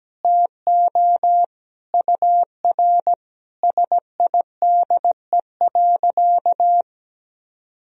Morse code, heard as a keyed tone, about 17 wpm, 700 Hz. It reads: TO UR SIDE.
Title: Sentences from Top 400 Words and Abbreviations 17wpm